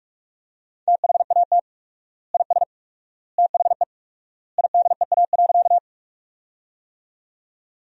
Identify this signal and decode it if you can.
Morse code, heard as a keyed tone, about 41 wpm, 700 Hz. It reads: THAT IS THE IDEA.
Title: 4x Word-Spacing - Sentences from Top 300 Words - 45wpm